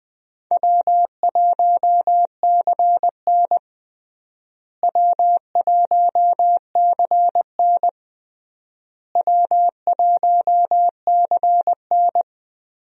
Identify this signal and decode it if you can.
Morse code, heard as a keyed tone, about 20 wpm, 700 Hz. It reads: W1CN W1CN W1CN
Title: USA Call Signs - 4 Characters - Repeated 3x - 20wpm